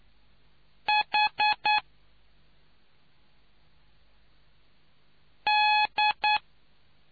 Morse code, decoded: H D

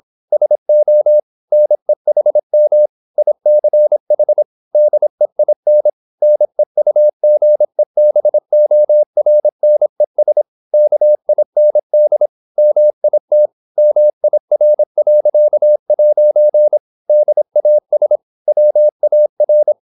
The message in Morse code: SO NEHM ICH DEIN NEUGEBORNES KIND MIT MIR.' DAS WAR

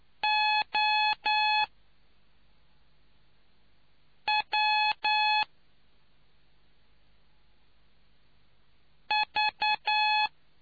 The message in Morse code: OWV

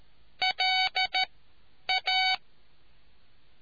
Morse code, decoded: LA

